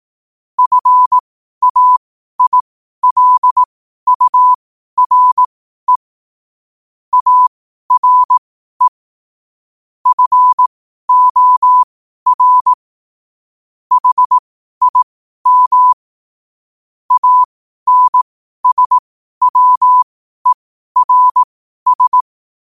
Morse code, heard as a keyed tone, about 18 wpm, spaced as for 12 wpm, 1000 Hz. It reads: FAILURE ARE FOR HIM ANSWERS
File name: qotdfiles / 12wpm